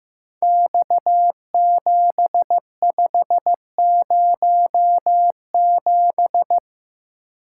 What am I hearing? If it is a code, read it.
Morse code, heard as a keyed tone, about 15 wpm, 700 Hz. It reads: X7507
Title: Code Groups - 5 Letter:Numbers - 15wpm